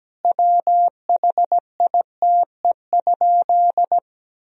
Morse code, heard as a keyed tone, about 17 wpm, 700 Hz. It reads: WHITE?
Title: Sentences from Top 1000 Words - Review Part 2 - 17wpm